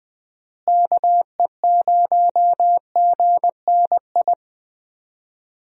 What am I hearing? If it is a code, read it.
Morse code, heard as a keyed tone, about 20 wpm, 700 Hz. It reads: KE0GNI
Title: USA Call Signs - 6 Characters - Repeated 2x - 20wpm